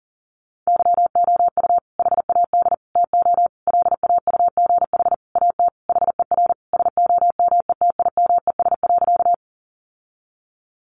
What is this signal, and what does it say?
Morse code, heard as a keyed tone, about 40 wpm, 700 Hz. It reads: YOU HAD TO LAUGH AT HER SOMETIMES.
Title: Sentences from Top 600 Words 40wpm